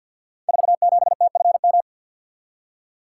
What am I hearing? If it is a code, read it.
Morse code, heard as a keyed tone, about 48 wpm, 700 Hz. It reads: 47TFM
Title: Code Groups - 5 Letter:Numbers - 50wpm